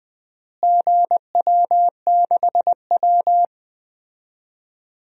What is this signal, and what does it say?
Morse code, heard as a keyed tone, about 20 wpm, 700 Hz. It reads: GW6W
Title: DXCC-100 Call Signs - 4 Characters - Repeated 2x - 20wpm